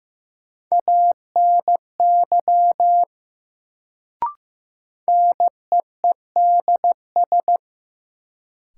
Morse code, heard as a keyed tone, about 15 wpm, 700 Hz. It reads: ANY NEEDS
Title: Sets of 3 Words - New Word Drill - 15wpm